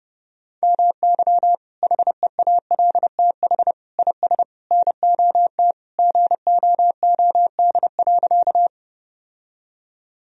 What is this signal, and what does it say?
Morse code, heard as a keyed tone, about 30 wpm, 700 Hz. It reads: MY HEALTH IS NOT GOOD.
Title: Sentences from Top 1000 Words - Review Part 3 - 30wpm